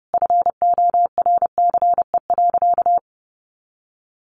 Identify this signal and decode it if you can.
Morse code, heard as a keyed tone, about 30 wpm, 700 Hz. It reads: FORCE.
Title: Sentences from Top 1200 Words 30wpm